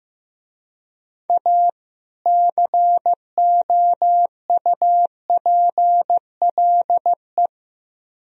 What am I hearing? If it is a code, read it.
Morse code, heard as a keyed tone, about 15 wpm, 700 Hz. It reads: A COUPLE
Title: Speed-Racing - Sets of 2 Words 15wpm